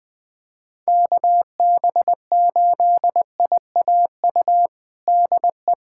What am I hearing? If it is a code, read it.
Morse code, heard as a keyed tone, about 20 wpm, 700 Hz. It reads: KB8IAU DE
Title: USA Callsign DE Callsign - 6 Characters - 20wpm